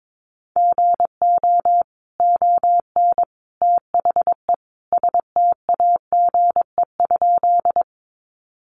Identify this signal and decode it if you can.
Morse code, heard as a keyed tone, about 22 wpm, 700 Hz. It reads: GO ON THE STAGE?
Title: Sentences from Top 1200 Words 22wpm